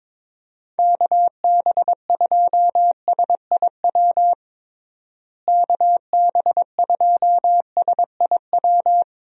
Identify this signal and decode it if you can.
Morse code, heard as a keyed tone, about 22 wpm, 700 Hz. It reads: KB2SIW KB2SIW
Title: USA Call Signs - 6 Characters - Repeated 3x - 22wpm